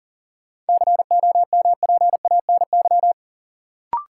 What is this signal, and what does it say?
Morse code, heard as a keyed tone, about 40 wpm, 700 Hz. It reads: COMPANY
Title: Sets of 2 Words - New Word Drill - 40wpm